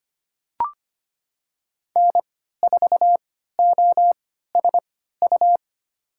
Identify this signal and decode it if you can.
Morse code, heard as a keyed tone, about 25 wpm, spaced as for 14 wpm, 700 Hz. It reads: N4OSU
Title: USA Call Signs - 5 Characters - 3x Character Spacing - 25wpm